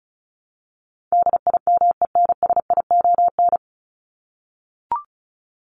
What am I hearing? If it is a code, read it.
Morse code, heard as a keyed tone, about 34 wpm, 700 Hz. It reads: DIMENSION
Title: Top 1501-1750 Words 35wpm